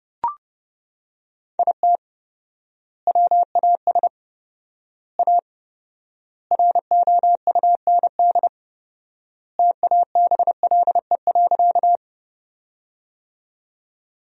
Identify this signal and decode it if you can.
Morse code, heard as a keyed tone, about 30 wpm, 700 Hz. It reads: IT WAS A ROUND TABLE.
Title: 4x Word-Spacing - Sentences from Top 400 Words - 30wpm